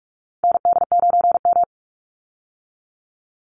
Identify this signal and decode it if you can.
Morse code, heard as a keyed tone, about 44 wpm, 700 Hz. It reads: ND9M
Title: Call Signs Repeated 2x - 45wpm